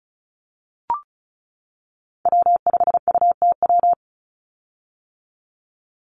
Morse code, heard as a keyed tone, about 34 wpm, 700 Hz. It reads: W5UTW
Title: Call Signs - 35wpm